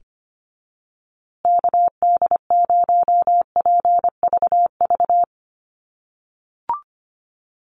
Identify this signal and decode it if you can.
Morse code, heard as a keyed tone, about 25 wpm, 700 Hz. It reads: KD0PVV
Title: USA Call Signs Repeated 3x 25wpm